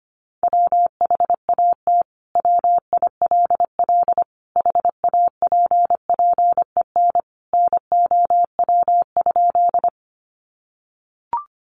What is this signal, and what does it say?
Morse code, heard as a keyed tone, about 25 wpm, 700 Hz. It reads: WHAT WILL HAPPEN NOW?
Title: Sentences from Top 400 Words 25wpm